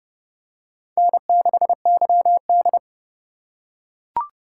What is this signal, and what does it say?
Morse code, heard as a keyed tone, about 26 wpm, 700 Hz. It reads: N6YD